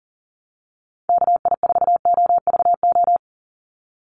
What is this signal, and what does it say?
Morse code, heard as a keyed tone, about 40 wpm, 690 Hz. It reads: KI4OVO
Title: USA Call Signs 40wpm